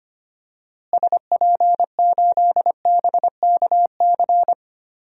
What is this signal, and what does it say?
Morse code, heard as a keyed tone, about 25 wpm, 700 Hz. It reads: SP8BKC
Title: DXCC-100 Call Signs - 6 Characters - Repeated 2x - 25wpm